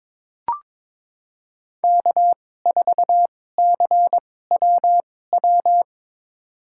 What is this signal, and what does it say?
Morse code, heard as a keyed tone, about 22 wpm, spaced as for 16 wpm, 700 Hz. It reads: K4CWW
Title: USA Call Signs - 5 Characters - 2x Character Spacing - 22wpm